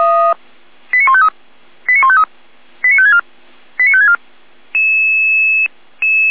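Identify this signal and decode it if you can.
Five-tone selective call (ZVEI), heard as quick runs of five tones, tones of about 70 ms. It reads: 89153 89153 89564 89564
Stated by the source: ZVEI five-tone call received by SDR.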